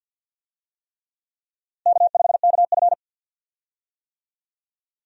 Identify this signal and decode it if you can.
Morse code, heard as a keyed tone, about 46 wpm, 700 Hz. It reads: K5KF
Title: USA Call Signs - 4 Characters - Repeated 3x - 50wpm